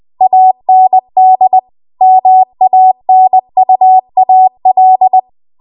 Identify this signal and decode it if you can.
Morse code, heard as a keyed tone, about 20 wpm, 750 Hz. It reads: AND MANUAL
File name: Archive / 20 WPM / 20WPM